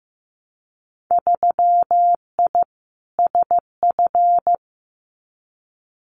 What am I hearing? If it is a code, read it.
Morse code, heard as a keyed tone, about 15 wpm, 700 Hz. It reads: 3I SF
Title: Winter Field Day Exchanges 15wpm